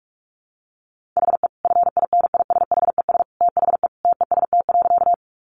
Morse code, heard as a keyed tone, about 44 wpm, 700 Hz. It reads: HE FINISHES THE TEST.
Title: Additional Easy Phrases 45wpm